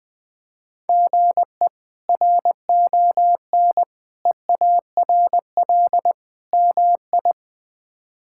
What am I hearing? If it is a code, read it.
Morse code, heard as a keyed tone, about 20 wpm, 700 Hz. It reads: GE RON EARL MI
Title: K1USN SST QSO Practice - 20wpm